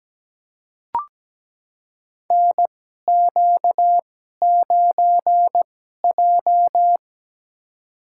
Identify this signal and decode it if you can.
Morse code, heard as a keyed tone, about 17 wpm, spaced as for 12 wpm, 700 Hz. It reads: NQ9J
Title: USA Call Signs - 2x Character Spacing - 17wpm